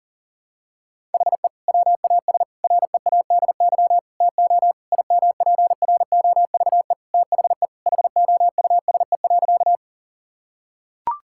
Morse code, heard as a keyed tone, about 37 wpm, 700 Hz. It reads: HE WAS READY TO IMPROVE THE HOUSE.
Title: Sentences from Top 1000 Words - Review Part 5 - 40wpm